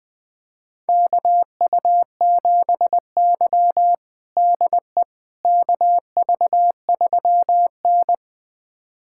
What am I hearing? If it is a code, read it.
Morse code, heard as a keyed tone, about 20 wpm, 700 Hz. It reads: KU7Y DE KV3N
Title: USA Callsign DE Callsign - 4 Characters - 20wpm